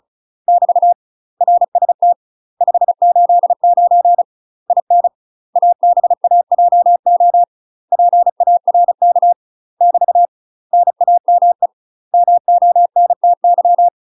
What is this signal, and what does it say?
Morse code, heard as a keyed tone, about 36 wpm, 700 Hz. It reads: <BT> RST 589 IN ABAJO PARK <BT> NAME MONTY
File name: UK MASTER QSOs TRAINING / 35WPM / UK MASTER QSOs 0022